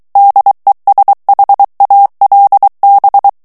Morse code, heard as a keyed tone, about 23 wpm, 800 Hz. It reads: DESHALB
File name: morse mp3 / ALLEMAND texte clair 1 vitesse 22 mots minute